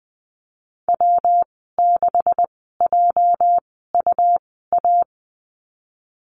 Morse code, heard as a keyed tone, about 20 wpm, spaced as for 14 wpm, 700 Hz. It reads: W6JUA
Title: 213 Lesson 34g - Number 8 - Call Signs 20wpm